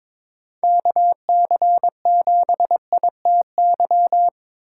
Morse code, heard as a keyed tone, about 25 wpm, 700 Hz. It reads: KC7ITY
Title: Speed Racing - USA 2x3 Call Signs 22wpm